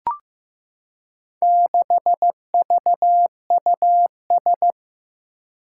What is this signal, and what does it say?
Morse code, heard as a keyed tone, about 15 wpm, 700 Hz. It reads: 6VUS
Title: Four Challenging Letter-Number Combinations 15wpm